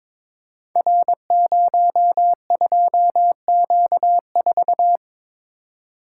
Morse code, heard as a keyed tone, about 22 wpm, 690 Hz. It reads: R02Q4
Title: Code Groups - 5 Letter:Numbers - 22wpm